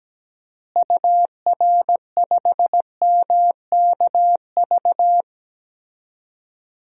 Morse code, heard as a keyed tone, about 17 wpm, 700 Hz. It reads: UR5MKV
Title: DXCC-100 Call Signs - 6 Characters - Repeated 2x - 17wpm